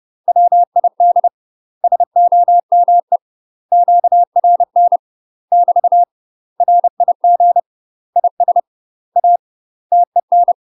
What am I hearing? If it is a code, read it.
Morse code, heard as a keyed tone, about 30 wpm, 700 Hz. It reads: WID SOME QRN <BT> RIG IS A TEN
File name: UK MASTER QSOs TRAINING / 30WPM / UK MASTER QSOs 0046